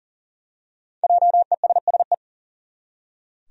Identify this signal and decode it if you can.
Morse code, heard as a keyed tone, about 38 wpm, 700 Hz. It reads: JESSE
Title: USA Common Names - Max 5 Letters - 40wpm